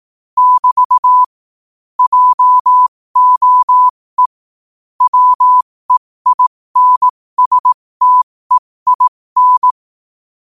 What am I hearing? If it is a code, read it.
Morse code, heard as a keyed tone, about 18 wpm, spaced as for 15 wpm, 1000 Hz. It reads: <BT> JOE WEINSTEIN
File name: qotdfiles / 15wpm